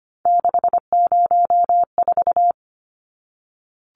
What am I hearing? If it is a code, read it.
Morse code, heard as a keyed tone, about 25 wpm, 700 Hz. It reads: 604
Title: Numbers - 3-Digit 25wpm